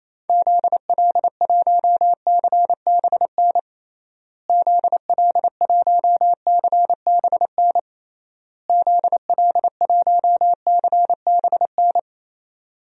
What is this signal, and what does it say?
Morse code, heard as a keyed tone, about 28 wpm, 700 Hz. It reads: ZL1CBN ZL1CBN ZL1CBN
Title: DXCC-100 Call Signs - 6 Characters - Repeated 3x - 28wpm